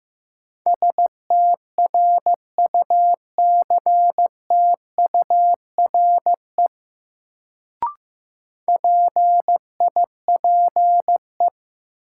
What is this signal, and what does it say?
Morse code, heard as a keyed tone, about 15 wpm, 700 Hz. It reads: STRUCTURE PIPE
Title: Sentences from Top 1500 Words - New Word Drill Alt - 15wpm